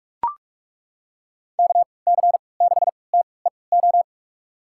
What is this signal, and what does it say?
Morse code, heard as a keyed tone, about 42 wpm, spaced as for 25 wpm, 700 Hz. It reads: KC6TEO
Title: USA Call Signs - 6 Characters - 3x Character Spacing - 45wpm